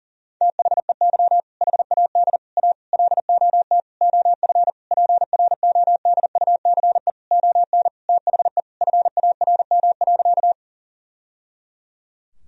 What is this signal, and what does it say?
Morse code, heard as a keyed tone, about 40 wpm, 700 Hz. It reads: THEY HAD A LOT OF PRODUCE ON THE FARM.
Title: Sentences from Top 1000 Words - Review Part 3 - 40wpm